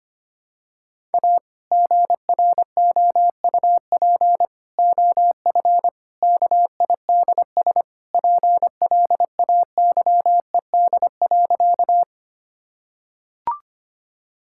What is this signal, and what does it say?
Morse code, heard as a keyed tone, about 25 wpm, 700 Hz. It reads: A GROUP OF KIDS PLAYED.
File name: practice / 25wpm / Sentences with 3-Word Phrases - 25wpm